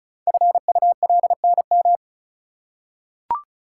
Morse code, heard as a keyed tone, about 36 wpm, 690 Hz. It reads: FULNM
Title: Code Groups - 5 Letters - 35wpm